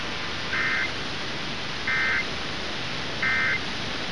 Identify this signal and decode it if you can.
Emergency Alert System (SAME) tones, heard as harsh buzzing bursts. It is heard from 0.4 to 3.7 s.